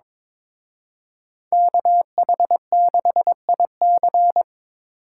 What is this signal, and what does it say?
Morse code, heard as a keyed tone, about 26 wpm, 700 Hz. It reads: KH6IC
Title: Speed Racing - USA 2x2 Call Signs 22wpm